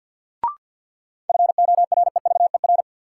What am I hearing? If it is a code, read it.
Morse code, heard as a keyed tone, about 52 wpm, 700 Hz. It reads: FOREVER
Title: Sentences from Top 1600 Words - New Word Drill - 50wpm